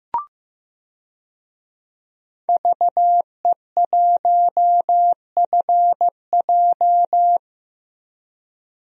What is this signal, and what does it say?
Morse code, heard as a keyed tone, about 15 wpm, 700 Hz. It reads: VE1FJ
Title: DXCC-100 Call Signs - 5 Characters - 15wpm